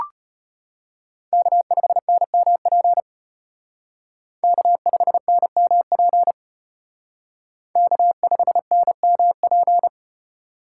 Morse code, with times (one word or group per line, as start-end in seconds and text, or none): 1.3-3.0 K5NMP
4.4-6.3 K5NMP
7.8-9.9 K5NMP